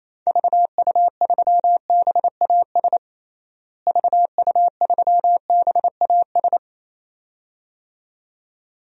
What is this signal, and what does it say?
Morse code, heard as a keyed tone, about 28 wpm, 700 Hz. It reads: VU3BAS VU3BAS
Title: DXCC-100 Call Signs - 6 Characters - Repeated 2x - 28wpm